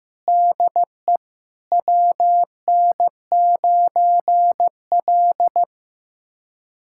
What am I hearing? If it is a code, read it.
Morse code, heard as a keyed tone, about 15 wpm, 700 Hz. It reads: DE WN9L
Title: USA Callsign DE Callsign 15wpm